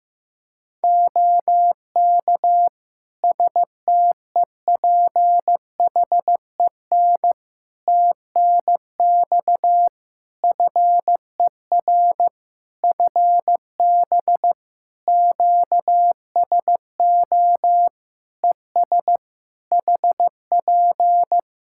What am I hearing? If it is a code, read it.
Morse code, heard as a keyed tone, about 16 wpm, 700 Hz. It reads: OK STEPHEN TNX FER FB QSO ES HP
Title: K1USN SST QSO Practice 15wpm